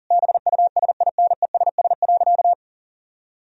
Morse code, heard as a keyed tone, about 40 wpm, 700 Hz. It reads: BUSINESS.
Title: Sentences from Top 1000 Words - Review Part 2 - 40wpm